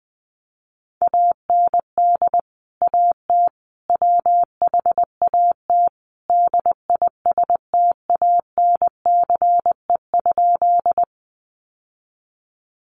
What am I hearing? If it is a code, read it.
Morse code, heard as a keyed tone, about 20 wpm, 700 Hz. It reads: AND AT WHAT DISTANCE?
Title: Sentences from Top 1200 Words 20wpm